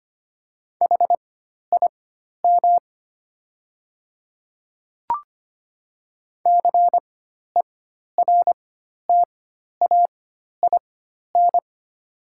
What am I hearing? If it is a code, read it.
Morse code, heard as a keyed tone, about 25 wpm, spaced as for 12 wpm, 700 Hz. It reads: HIM CERTAIN
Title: 4x Character Spacing - Top 500 Words - 25wpm